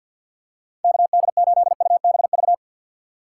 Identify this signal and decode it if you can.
Morse code, heard as a keyed tone, about 52 wpm, 700 Hz. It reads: KD8UBV
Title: USA Call Signs - 6 Characters - 4x Character Spacing - 50wpm